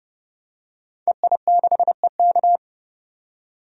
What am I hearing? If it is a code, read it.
Morse code, heard as a keyed tone, about 30 wpm, 700 Hz. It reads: EI6EK
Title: DXCC-100 Call Signs - 5 Characters - 30wpm